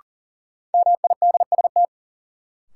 Morse code, heard as a keyed tone, about 40 wpm, 700 Hz. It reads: MIDST
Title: Sets of 4 Words - New Word Drill - 40wpm